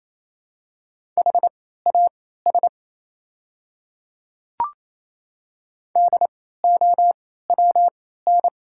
Morse code, heard as a keyed tone, about 28 wpm, spaced as for 16 wpm, 700 Hz. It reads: HAS DOWN
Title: 3x Character Spacing - Top 100 Words - 28wpm